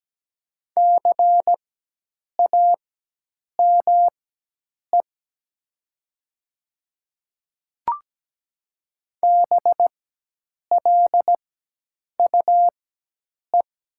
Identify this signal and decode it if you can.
Morse code, heard as a keyed tone, about 17 wpm, spaced as for 8 wpm, 700 Hz. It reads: CAME BLUE
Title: 4x Character Spacing - Top 500 Words - 17wpm